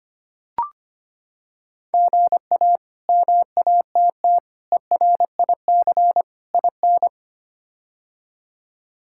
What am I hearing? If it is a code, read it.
Morse code, heard as a keyed tone, about 25 wpm, 700 Hz. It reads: GA MATT ERIC IN
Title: K1USN SST QSO Practice - 25wpm